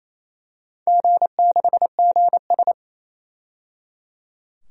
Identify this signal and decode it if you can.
Morse code, heard as a keyed tone, about 28 wpm, 700 Hz. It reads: G6GS
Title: DXCC-100 Call Signs - 4 Characters - 28wpm